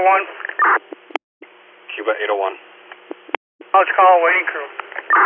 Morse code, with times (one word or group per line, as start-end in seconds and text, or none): none